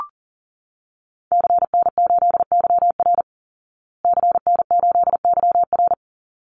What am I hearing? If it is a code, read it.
Morse code, heard as a keyed tone, about 40 wpm, 700 Hz. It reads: CN8YR CN8YR
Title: Call Signs Repeated 3x - 40wpm